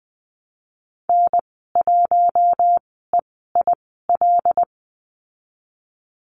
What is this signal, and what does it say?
Morse code, heard as a keyed tone, about 20 wpm, spaced as for 15 wpm, 700 Hz. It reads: N1EIL
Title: 076 Lesson 13g - Letter L - Call Signs 20wpm